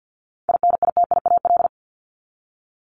Morse code, heard as a keyed tone, about 50 wpm, 700 Hz. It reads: INITIAL
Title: Top 1501-1750 Words 50wpm